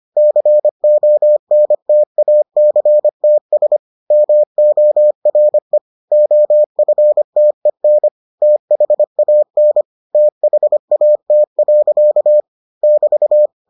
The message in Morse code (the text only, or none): CONTACTS MORE OFTEN THAN THAT. <BT>